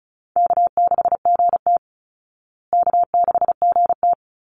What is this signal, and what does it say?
Morse code, heard as a keyed tone, about 34 wpm, 700 Hz. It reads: K6GT K6GT
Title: Easy Call Signs Repeated 2x - 35wpm